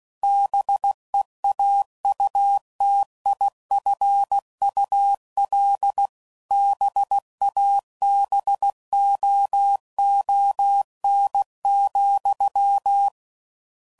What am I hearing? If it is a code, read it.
Morse code, heard as a keyed tone, about 16 wpm, 800 Hz. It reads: BEAUTIFUL BABOON,